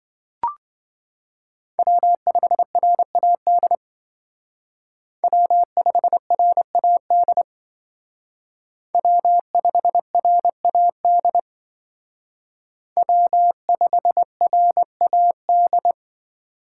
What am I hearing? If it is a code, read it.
Morse code, heard as a keyed tone, about 24 wpm, 700 Hz. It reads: W5RAD W5RAD W5RAD W5RAD
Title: Speed Racing - USA 1x3 Call Signs 20wpm